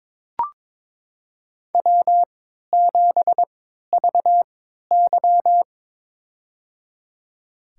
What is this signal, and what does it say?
Morse code, heard as a keyed tone, about 21 wpm, spaced as for 12 wpm, 700 Hz. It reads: W7VY
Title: USA Call Signs - 4 Characters - 3x Character Spacing - 22wpm